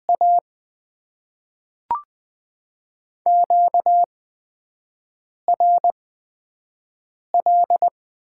Morse code, heard as a keyed tone, about 20 wpm, 700 Hz. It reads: A Q R L
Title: Top 20 QSO Elements - 8x Character Spacing - 20wpm